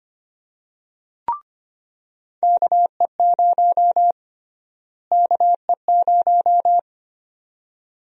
Morse code, heard as a keyed tone, about 25 wpm, 700 Hz. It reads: KE0 KE0
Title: USA Call Sign Prefixes - Repeated 2x - 25wpm